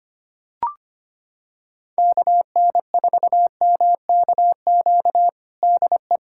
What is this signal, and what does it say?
Morse code, heard as a keyed tone, about 25 wpm, 700 Hz. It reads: KN4MKQ DE
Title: USA Callsign DE Callsign - 6 Characters - 25wpm